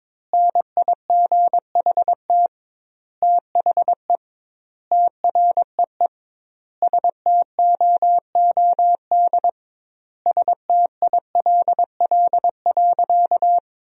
Morse code, heard as a keyed tone, about 22 wpm, 700 Hz. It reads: NIGHT THE TREE STOOD STILL.
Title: 2x Word-Spacing - Sentences from Top 500 Words - 22wpm